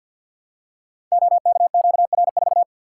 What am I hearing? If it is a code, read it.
Morse code, heard as a keyed tone, about 52 wpm, 700 Hz. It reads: OKQR4
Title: Code Groups - 5 Letter:Numbers - 50wpm